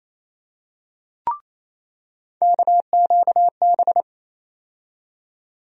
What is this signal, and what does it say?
Morse code, heard as a keyed tone, about 28 wpm, 700 Hz. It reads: KQB
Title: USA Call Sign Postfixes - 3 Letters 28wpm